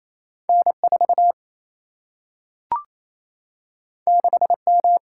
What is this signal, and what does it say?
Morse code, heard as a keyed tone, about 28 wpm, 700 Hz. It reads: N4 6M
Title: Single-Letter Single-Number 28wpm